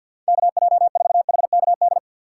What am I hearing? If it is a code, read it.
Morse code, heard as a keyed tone, about 64 wpm, 700 Hz. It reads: KJ4HKD